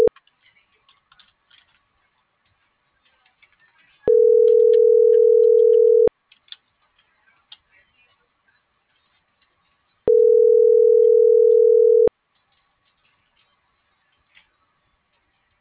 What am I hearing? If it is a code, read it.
Ringback tone (North America), 440 + 480 Hz.